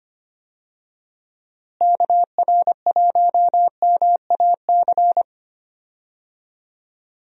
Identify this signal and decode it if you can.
Morse code, heard as a keyed tone, about 25 wpm, 700 Hz. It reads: KR1MAC
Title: USA Call Signs - 6 Characters - Repeated 3x - 25wpm